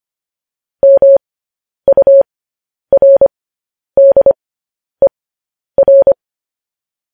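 Morse code, heard as a keyed tone, about 25 wpm, spaced as for 10 wpm, 550 Hz. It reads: MURDER
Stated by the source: Morse code news headlines sent at 10 wpm.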